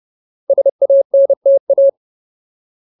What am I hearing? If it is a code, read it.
Morse code, heard as a keyed tone, about 30 wpm, 550 Hz. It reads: SANTA
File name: Ultimate Word List / 30WPM Ultimate